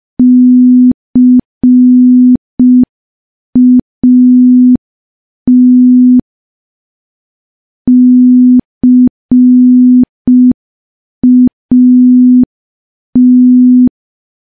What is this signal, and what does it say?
Morse code, heard as a keyed tone, about 5 wpm, 250 Hz. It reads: CAT CAT
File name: morse / morse-training-05WPM